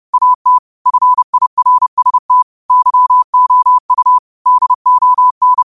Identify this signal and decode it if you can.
Morse code, heard as a keyed tone, about 30 wpm, 1000 Hz. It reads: AT FIRST YOU DON
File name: qotdfiles / 30wpm